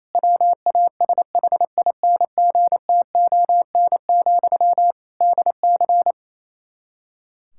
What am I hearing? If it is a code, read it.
Morse code, heard as a keyed tone, about 28 wpm, 700 Hz. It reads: WASHINGTON, DC
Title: K1USN SST QSO Practice 28wpm